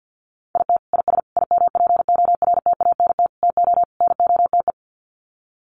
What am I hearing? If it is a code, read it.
Morse code, heard as a keyed tone, about 48 wpm, 700 Hz. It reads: IT IS IMPORTANT TO NOTE
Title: Sets of 5 Words 50wpm